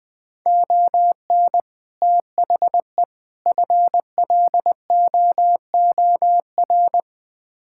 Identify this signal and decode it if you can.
Morse code, heard as a keyed tone, about 20 wpm, 700 Hz. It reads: ON THE FLOOR
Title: Build A Phrase - 3-Word Phrases - 20wpm